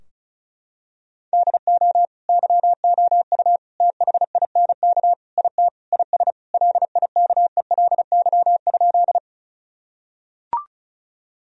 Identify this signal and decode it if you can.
Morse code, heard as a keyed tone, about 36 wpm, 700 Hz. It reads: DO YOU THINK IT IS LIKELY?
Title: Sentences from Top 1000 Words - Review Part 4 - 35wpm